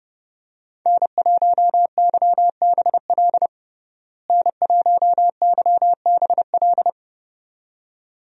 Morse code, heard as a keyed tone, about 32 wpm, 700 Hz. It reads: N1YBL N1YBL
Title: USA Call Signs - 5 Characters - Repeated 2x - 30wpm